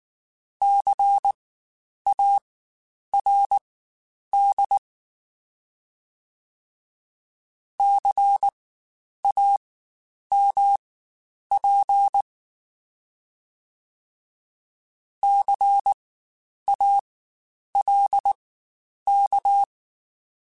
Morse code, heard as a keyed tone, about 19 wpm, spaced as for 9 wpm, 800 Hz. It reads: CARD CAMP CALK
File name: Spaced Alphas / Spaced Alphas mp3s / Cs sp @18